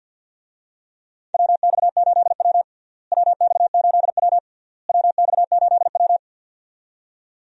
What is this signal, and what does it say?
Morse code, heard as a keyed tone, about 47 wpm, 700 Hz. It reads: WX8W WX8W WX8W
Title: USA Call Signs - 4 Characters - Repeated 3x - 50wpm